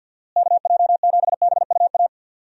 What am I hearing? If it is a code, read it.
Morse code, heard as a keyed tone, about 50 wpm, 700 Hz. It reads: KJ7BUA
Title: USA Call Signs - 6 Characters - Repeated 3x - 50wpm